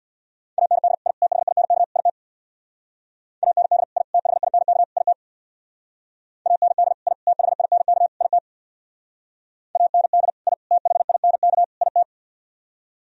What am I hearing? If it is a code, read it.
Morse code, heard as a keyed tone, about 62 wpm, 700 Hz. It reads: AND I THINK IT AND I THINK IT AND I THINK IT AND I THINK IT
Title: Speed-Racing - Sets of 4 Words 50wpm